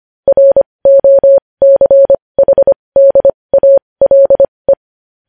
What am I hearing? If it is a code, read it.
Morse code, heard as a keyed tone, about 25 wpm, spaced as for 20 wpm, 550 Hz. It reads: ROCHDALE
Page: News Headlines in Morse Code at 20 WPM